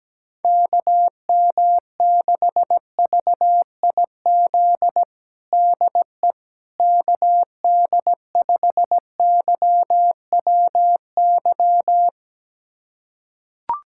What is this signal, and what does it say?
Morse code, heard as a keyed tone, about 17 wpm, 700 Hz. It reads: KM6VIZ DE KD5YWY
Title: USA Callsign DE Callsign - 6 Characters - 17wpm